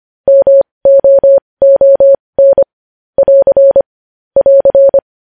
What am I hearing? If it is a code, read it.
Morse code, heard as a keyed tone, about 25 wpm, spaced as for 20 wpm, 550 Hz. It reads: MOON <AR> <AR>
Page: News Headlines in Morse Code at 20 WPM